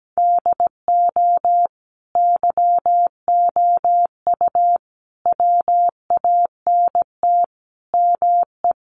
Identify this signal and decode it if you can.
Morse code, heard as a keyed tone, about 17 wpm, 700 Hz. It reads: DO YOU WANT ME
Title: Sets of 5 Words 17wpm